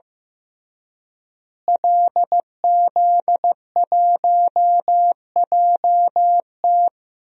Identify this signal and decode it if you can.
Morse code, heard as a keyed tone, about 15 wpm, 700 Hz. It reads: LZ1JT
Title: DXCC-100 Call Signs - 5 Characters - Repeated 3x - 15wpm